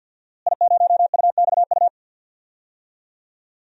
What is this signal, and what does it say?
Morse code, heard as a keyed tone, about 50 wpm, 700 Hz. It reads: I0UXU
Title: DXCC-100 Call Signs - 5 Characters - Repeated 2x - 50wpm